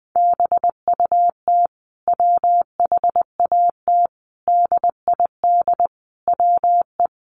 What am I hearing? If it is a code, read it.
Morse code, heard as a keyed tone, about 20 wpm, 700 Hz. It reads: BUT WHAT DID WE
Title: Sentences from Top 400 Words 20wpm